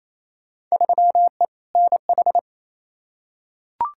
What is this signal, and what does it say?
Morse code, heard as a keyed tone, about 28 wpm, 700 Hz. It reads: 3E NH
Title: Field Day Exchanges Encore - 28wpm